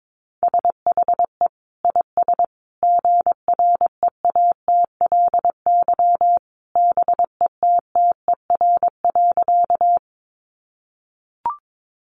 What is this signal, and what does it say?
Morse code, heard as a keyed tone, about 22 wpm, 700 Hz. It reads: SHE IS GREATLY BETTER.
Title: Sentences from Top 1400 Words 22wpm